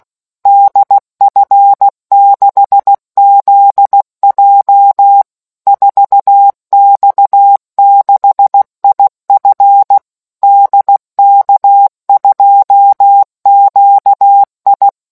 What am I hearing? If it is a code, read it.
Morse code, heard as a keyed tone, about 16 wpm, 800 Hz. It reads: DF6ZJ 4X6IF DK2QI